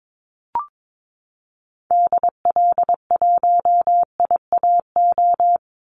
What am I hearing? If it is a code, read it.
Morse code, heard as a keyed tone, about 22 wpm, 700 Hz. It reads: DL1IAO
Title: Call Signs Repeated 2x - 22wpm